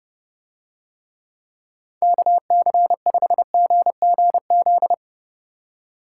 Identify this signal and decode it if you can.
Morse code, heard as a keyed tone, about 30 wpm, 700 Hz. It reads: KC5GGZ